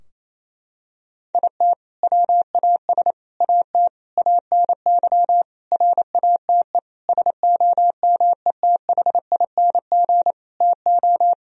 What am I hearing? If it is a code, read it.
Morse code, heard as a keyed tone, about 27 wpm, 700 Hz. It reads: IT WAS AT ANY RATE SOMETHING TO
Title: Sentences from Top 1000 Words - Review Part 4 - 28wpm